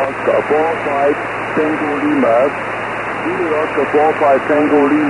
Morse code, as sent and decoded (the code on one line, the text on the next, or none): none